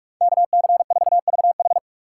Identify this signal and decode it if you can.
Morse code, heard as a keyed tone, about 43 wpm, 700 Hz. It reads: KC4VH